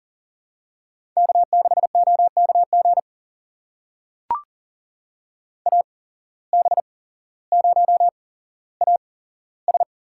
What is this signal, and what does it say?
Morse code, heard as a keyed tone, about 40 wpm, spaced as for 17 wpm, 700 Hz. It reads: K6OKG AB0AS
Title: USA Call Signs - 5 Characters - 8x Character Spacing - 40wpm